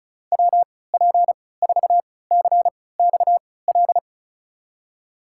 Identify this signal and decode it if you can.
Morse code, heard as a keyed tone, about 34 wpm, spaced as for 20 wpm, 700 Hz. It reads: WP4CXL
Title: USA Call Signs - 6 Characters - 3x Character Spacing - 35wpm